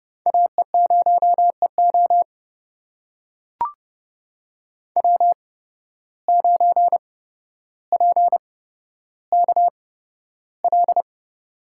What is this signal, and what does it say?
Morse code, heard as a keyed tone, about 30 wpm, spaced as for 13 wpm, 700 Hz. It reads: AE0EO W9PKL
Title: USA Call Signs - 5 Characters - 8x Character Spacing - 30wpm